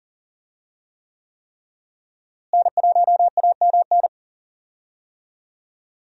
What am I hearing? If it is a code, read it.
Morse code, heard as a keyed tone, about 38 wpm, 700 Hz. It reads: N1AMN
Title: USA Call Signs - 5 Characters - Repeated 3x - 40wpm